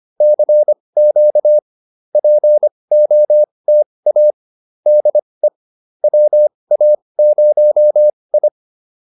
Morse code, heard as a keyed tone, about 25 wpm, spaced as for 20 wpm, 600 Hz. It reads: CQ POTA DE WA0I